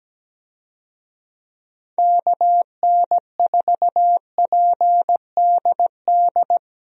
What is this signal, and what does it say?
Morse code, heard as a keyed tone, about 17 wpm, 700 Hz. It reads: KN4PDD
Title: USA Call Signs - 6 Characters - Repeated 2x - 17wpm